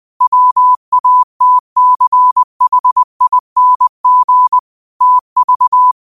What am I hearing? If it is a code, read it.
Morse code, heard as a keyed tone, about 20 wpm, 1000 Hz. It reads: WATCHING TV